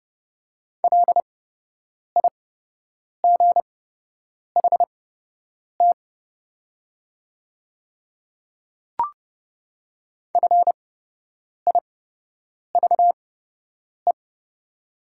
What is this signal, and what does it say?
Morse code, heard as a keyed tone, about 30 wpm, spaced as for 8 wpm, 700 Hz. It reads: LIGHT FIVE
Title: 8x Character Spacing - Top 500 Words - 30wpm